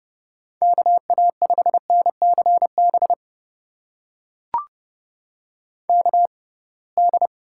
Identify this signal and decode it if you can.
Morse code, heard as a keyed tone, about 31 wpm, spaced as for 23 wpm, 700 Hz. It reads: KA5NCB KD
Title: USA Call Signs - 6 Characters - 6x Character Spacing - 30wpm